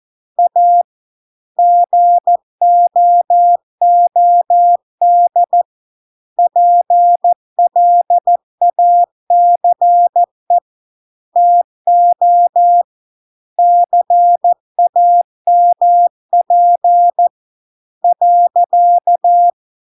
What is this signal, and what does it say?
Morse code, heard as a keyed tone, about 14 wpm, 700 Hz. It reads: A GOOD PLACE TO CAMP .